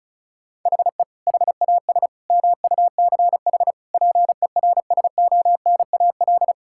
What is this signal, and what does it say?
Morse code, heard as a keyed tone, about 36 wpm, 700 Hz. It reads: HE HAS MUCH PERSONAL
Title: Sentences from Top 1000 Words - Review Part 5 - 35wpm